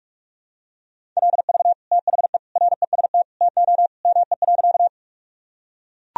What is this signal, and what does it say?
Morse code, heard as a keyed tone, about 43 wpm, 700 Hz. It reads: LV THE REST TO ME.
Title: Sentences from Top 500 Words and Abbreviations 45wpm